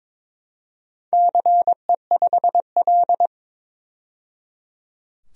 Morse code, heard as a keyed tone, about 23 wpm, 700 Hz. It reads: CE5L
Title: DXCC-100 Call Signs - 4 Characters - 22wpm